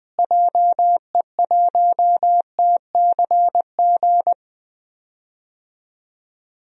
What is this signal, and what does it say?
Morse code, heard as a keyed tone, about 20 wpm, 700 Hz. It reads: JE1TCG
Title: DXCC-100 Call Signs - 6 Characters - Repeated 3x - 20wpm